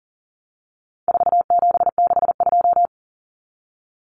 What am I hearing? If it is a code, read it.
Morse code, heard as a keyed tone, about 40 wpm, 700 Hz. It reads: 4762
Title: Numbers - 4-Digit 40wpm